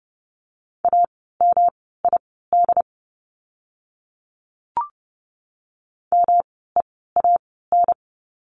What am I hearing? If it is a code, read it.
Morse code, heard as a keyed tone, about 30 wpm, spaced as for 17 wpm, 700 Hz. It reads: AMID MEAN
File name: course / 30:10wpm / 108 Lesson 18f - Letter m - Words 30wpm